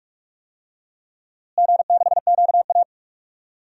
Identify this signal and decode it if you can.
Morse code, heard as a keyed tone, about 46 wpm, 700 Hz. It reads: G6QA
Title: DXCC-100 Call Signs - 4 Characters - Repeated 3x - 45wpm